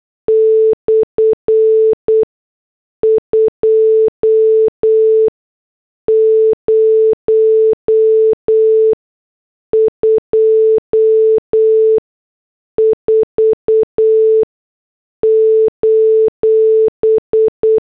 Morse code, heard as a keyed tone, about 8 wpm, spaced as for 6 wpm, 440 Hz. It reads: /2024: